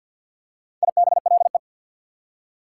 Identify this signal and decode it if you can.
Morse code, heard as a keyed tone, about 48 wpm, 690 Hz. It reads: IBLE
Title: Prefixes and Suffixes 50wpm